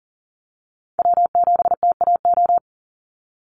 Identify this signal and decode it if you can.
Morse code, heard as a keyed tone, about 37 wpm, 700 Hz. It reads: W7TAO
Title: NA SOTA Chaser Callsigns 40wpm